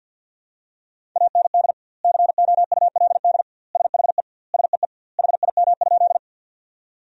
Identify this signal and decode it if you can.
Morse code, heard as a keyed tone, about 50 wpm, 700 Hz. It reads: AND COULD SHE SEE HIM?
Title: 2x Word-Spacing - Sentences from Top 100 Words - 50wpm